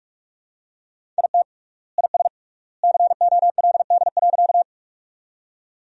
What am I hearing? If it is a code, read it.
Morse code, heard as a keyed tone, about 42 wpm, 700 Hz. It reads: IT IS COLD.
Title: 3x Word-Spacing - Sentences from Top 400 Words - 45wpm